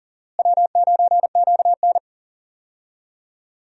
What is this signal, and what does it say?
Morse code, heard as a keyed tone, about 40 wpm, 700 Hz. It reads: W9QN
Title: USA Call Signs - 4 Characters - Repeated 2x - 40wpm